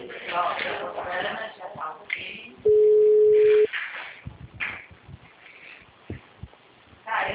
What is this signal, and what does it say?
Ringback tone (Europe, 425 Hz), with speech.
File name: out- -2001- - - .69